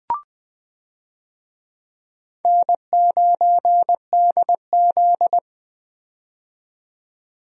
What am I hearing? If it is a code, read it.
Morse code, heard as a keyed tone, about 20 wpm, 700 Hz. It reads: N9DZ
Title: USA Call Signs - 4 Characters - 20wpm